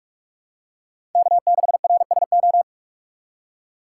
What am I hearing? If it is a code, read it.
Morse code, heard as a keyed tone, about 43 wpm, 700 Hz. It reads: K6RSO